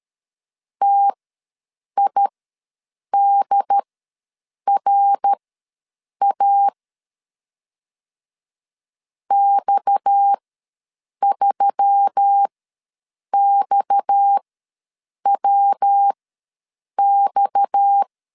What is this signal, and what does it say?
Morse code, heard as a keyed tone, about 13 wpm, spaced as for 7 wpm, 800 Hz. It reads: TIDRA X3XWX
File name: LEZIONI / 6 PKX0 / 6 PKX ws 12wpm 3106